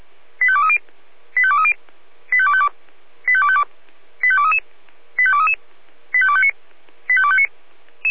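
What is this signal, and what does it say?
Five-tone selective call (ZVEI), heard as quick runs of five tones, tones of about 70 ms. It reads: 85230 85230 85252 85252 85233 85233 85259 85259
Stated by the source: ZVEI five-tone selective call 85232 received by SDR.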